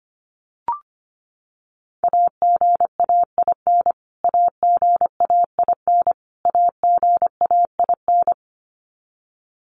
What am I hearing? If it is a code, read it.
Morse code, heard as a keyed tone, about 25 wpm, 700 Hz. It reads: AGAIN AGAIN AGAIN
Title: Top 500 Words Repeated 25wpm